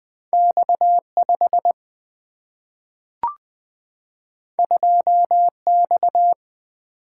Morse code, heard as a keyed tone, about 20 wpm, 700 Hz. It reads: X 5 2 X
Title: Single-Letter Single-Number 20wpm